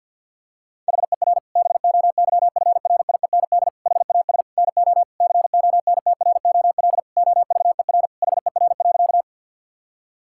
Morse code, heard as a keyed tone, about 50 wpm, 700 Hz. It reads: HER BOYFRIEND HAS NO CONTROL OVER HER.
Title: Sentences from Top 1000 Words - Review Part 5 - 50wpm